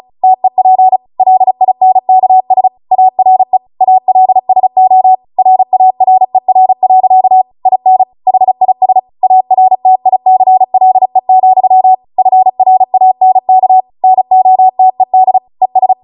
Morse code, heard as a keyed tone, about 36 wpm, 750 Hz. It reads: TEP LINKS ARE ALSO RARER. IN HIS ARTICLE, FRANK NOTED ES